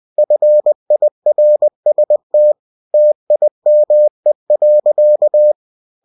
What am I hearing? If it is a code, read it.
Morse code, heard as a keyed tone, about 20 wpm, 600 Hz. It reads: FIRST TIME.
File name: cw-books / winnie-the-pooh / winnie-20wpm-page-0013